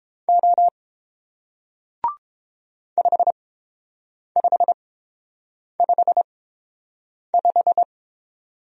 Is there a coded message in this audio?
Yes: Morse code from 0.3 to 7.8 s, heard as a keyed tone, about 29 wpm, spaced as for 7 wpm, 700 Hz. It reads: O 5555